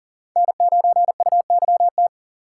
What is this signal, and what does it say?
Morse code, heard as a keyed tone, about 52 wpm, 700 Hz. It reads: N9UYT